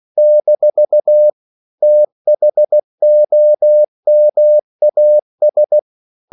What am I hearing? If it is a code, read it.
Morse code, heard as a keyed tone, about 16 wpm, 600 Hz. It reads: - THOMAS
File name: CW / Practice / EnglishQuotations0416